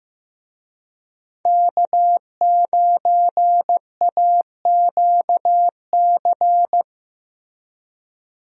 Morse code, heard as a keyed tone, about 15 wpm, 700 Hz. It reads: K9AQC
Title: USA Call Signs - 5 Characters - Repeated 2x - 15wpm